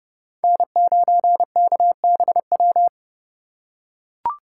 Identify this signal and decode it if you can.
Morse code, heard as a keyed tone, about 28 wpm, 700 Hz. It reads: N9KBW